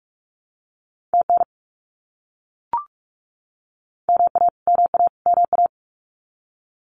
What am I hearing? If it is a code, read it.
Morse code, heard as a keyed tone, about 44 wpm, 700 Hz. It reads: TN MA MA MA
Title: US States Repeated 3x 45wpm